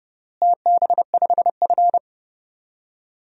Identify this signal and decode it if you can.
Morse code, heard as a keyed tone, about 30 wpm, 700 Hz. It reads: TB5F
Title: DXCC-100 Call Signs - 4 Characters - Repeated 3x - 30wpm